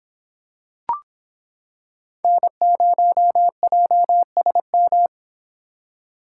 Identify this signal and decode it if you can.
Morse code, heard as a keyed tone, about 25 wpm, 700 Hz. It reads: N0JSM